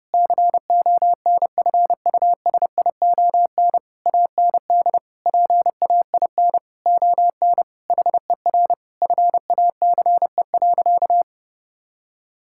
Morse code, heard as a keyed tone, about 30 wpm, 700 Hz. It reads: CONFUSION AND PAIN ON HER FACE.
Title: Sentences from Top 1600 Words 30wpm